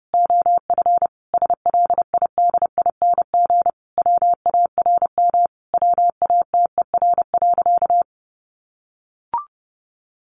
Morse code, heard as a keyed tone, about 30 wpm, 700 Hz. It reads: OF SLIDING WARM WATER.
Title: A Sound Of Thunder 30wpm